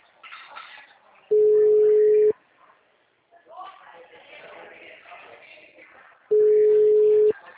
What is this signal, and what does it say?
Ringback tone (Europe, 425 Hz), with speech.